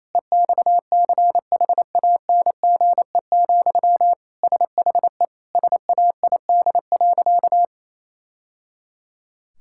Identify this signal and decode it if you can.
Morse code, heard as a keyed tone, about 27 wpm, 700 Hz. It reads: EXCHANGE, SHE SAID.
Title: Sentences from Top 1000 Words - Review Part 6 - 28wpm